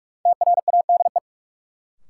Morse code, heard as a keyed tone, about 44 wpm, 700 Hz. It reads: TRADE
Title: Sentences from Top 600 Words - New Word Drill - 45wpm